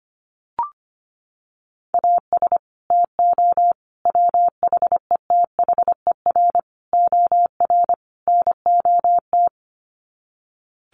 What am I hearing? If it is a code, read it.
Morse code, heard as a keyed tone, about 25 wpm, 700 Hz. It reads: AS TO WHETHER OR NOT
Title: Sets of 5 Words Encore 25wpm